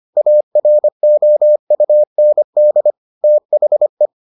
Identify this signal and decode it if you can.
Morse code, heard as a keyed tone, about 25 wpm, 600 Hz. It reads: AROUND THE